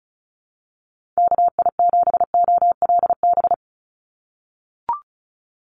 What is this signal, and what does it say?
Morse code, heard as a keyed tone, about 34 wpm, 700 Hz. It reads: KI7OLB
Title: USA Call Signs 35wpm